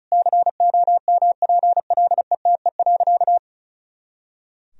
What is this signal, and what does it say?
Morse code, heard as a keyed tone, about 34 wpm, 700 Hz. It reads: COMPLETE.
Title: Sentences from Top 400 Words and Abbreviations 35wpm